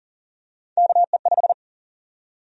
Morse code, heard as a keyed tone, about 40 wpm, 700 Hz. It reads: KE5
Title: USA Call Sign Prefixes - Repeated 2x - 40wpm